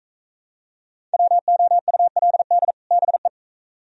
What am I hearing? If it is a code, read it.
Morse code, heard as a keyed tone, about 44 wpm, 700 Hz. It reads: WOULD BE